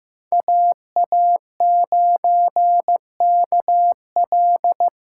Morse code, heard as a keyed tone, about 15 wpm, 700 Hz. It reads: AA9KL
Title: USA Call Signs - 5 Characters - 2x Character Spacing - 15wpm